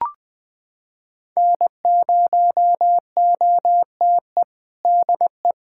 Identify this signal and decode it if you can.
Morse code, heard as a keyed tone, about 21 wpm, 700 Hz. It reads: N0OTE DE
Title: USA Callsign DE Callsign - 5 Characters - 20wpm